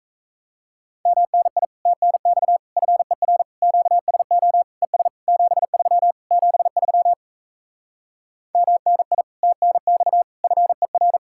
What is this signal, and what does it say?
Morse code, heard as a keyed tone, about 36 wpm, 700 Hz. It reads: MNI TNX FER QSO ES 73 73 MNI TNX FER